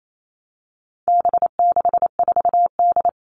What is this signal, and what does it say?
Morse code, heard as a keyed tone, about 28 wpm, 700 Hz. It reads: B64D
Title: All Cut Numbers - 4-Digit 28wpm